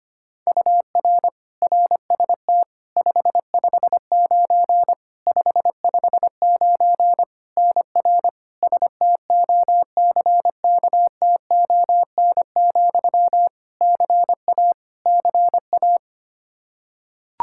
Morse code, heard as a keyed tone, about 25 wpm, 700 Hz. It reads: UR RST 559 559 NR STOCKTON, CA CA
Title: Practice QSO - Exchange 1 - Beginning - 25wpm